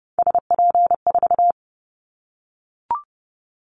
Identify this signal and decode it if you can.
Morse code, heard as a keyed tone, about 30 wpm, 690 Hz. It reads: SP4
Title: Callsign Prefixes for Top 25 DXCC - Repeated 3x - 30wpm